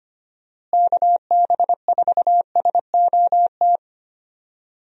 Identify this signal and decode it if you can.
Morse code, heard as a keyed tone, about 25 wpm, 700 Hz. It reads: KB4SOT
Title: USA Call Signs - 6 Characters - Repeated 2x - 25wpm